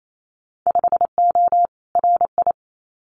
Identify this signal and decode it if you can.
Morse code, heard as a keyed tone, about 28 wpm, 700 Hz. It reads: 5O RI
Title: Winter Field Day Exchanges Encore 28wpm